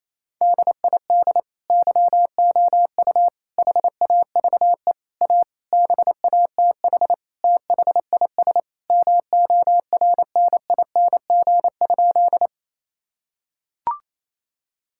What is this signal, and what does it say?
Morse code, heard as a keyed tone, about 28 wpm, 700 Hz. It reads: DID YOU HAVE A BATH THIS MORNING?
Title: Sentences from Top 1600 Words 28wpm